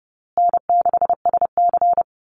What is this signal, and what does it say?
Morse code, heard as a keyed tone, about 30 wpm, 700 Hz. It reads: N6SC
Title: USA Call Signs Repeated 3x 30wpm